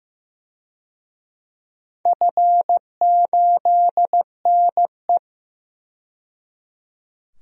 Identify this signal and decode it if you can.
Morse code, heard as a keyed tone, about 15 wpm, 700 Hz. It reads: F8NE
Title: DXCC-100 Call Signs - 4 Characters - 15wpm